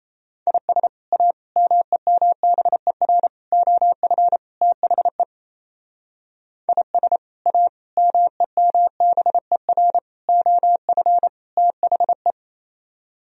Morse code, heard as a keyed tone, about 36 wpm, 700 Hz. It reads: IS A MEMBER OF THE IS A MEMBER OF THE